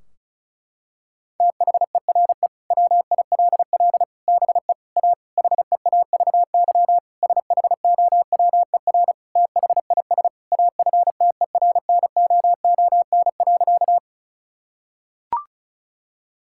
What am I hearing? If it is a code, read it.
Morse code, heard as a keyed tone, about 36 wpm, 700 Hz. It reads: THERE WILL BE A HEAVY SHOWER THIS AFTERNOON.
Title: Sentences from Top 1600 Words 35wpm